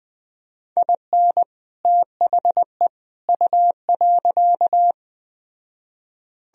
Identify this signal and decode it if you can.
Morse code, heard as a keyed tone, about 20 wpm, 700 Hz. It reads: IN THE U.